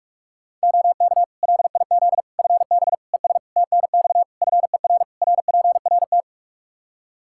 Morse code, heard as a keyed tone, about 41 wpm, 700 Hz. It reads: OK LIZ FB ES TNX FER RPRT
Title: K1USN SST QSO Practice 45wpm